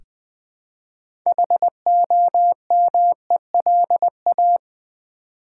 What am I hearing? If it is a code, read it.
Morse code, heard as a keyed tone, about 20 wpm, 700 Hz. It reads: HOMELA
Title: Build A Word - 3 Letters At A Time - 8-Letter Words - 20wpm